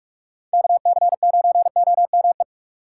Morse code, heard as a keyed tone, about 46 wpm, 690 Hz. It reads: KC9OME
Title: USA Call Signs - 6 Characters - 3x Character Spacing - 45wpm